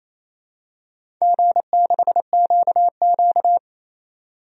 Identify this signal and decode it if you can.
Morse code, heard as a keyed tone, about 27 wpm, 690 Hz. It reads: G6QQ
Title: DXCC-100 Call Signs - 4 Characters - 28wpm